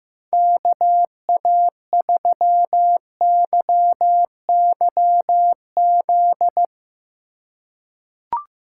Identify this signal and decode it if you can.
Morse code, heard as a keyed tone, about 15 wpm, 700 Hz. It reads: KA3YYZ
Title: USA Call Signs - 6 Characters - 4x Character Spacing - 15wpm